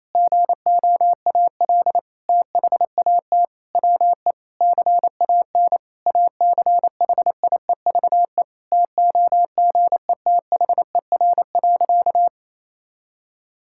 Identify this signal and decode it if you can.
Morse code, heard as a keyed tone, about 28 wpm, 700 Hz. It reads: GOAL THAT WE CAN ACHIEVE TOGETHER.
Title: Sentences from Top 1000 Words - Review Part 6 - 28wpm